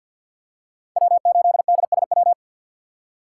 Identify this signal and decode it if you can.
Morse code, heard as a keyed tone, about 47 wpm, 700 Hz. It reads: W8DSW
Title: USA Call Signs - 5 Characters - 6x Character Spacing - 50wpm